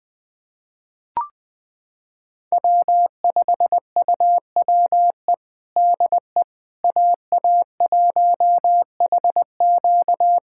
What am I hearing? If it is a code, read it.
Morse code, heard as a keyed tone, about 21 wpm, 700 Hz. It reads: W5UWE DE AA1HQ
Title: USA Callsign DE Callsign - 5 Characters - 20wpm